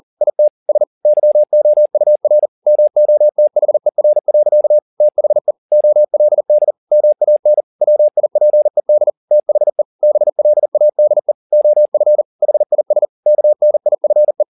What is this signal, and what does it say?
Morse code, heard as a keyed tone, about 40 wpm, 600 Hz. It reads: IT S YOUR MOTHER. THE OLD MAN WIPED THE BLADE OF HIS KNIFE
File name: the old man and the sea 2 40wpm